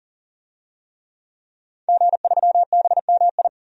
Morse code, heard as a keyed tone, about 37 wpm, 700 Hz. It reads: G3BMI
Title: DXCC-100 Call Signs - 5 Characters - Repeated 2x - 40wpm